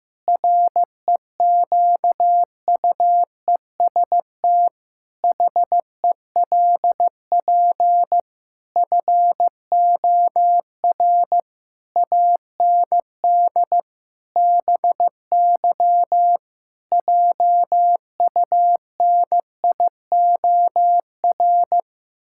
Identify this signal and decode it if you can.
Morse code, heard as a keyed tone, about 16 wpm, 700 Hz. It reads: REQUEST HELP FOR AND BY JUNIOR
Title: Pangram Challenge v2 - 15wpm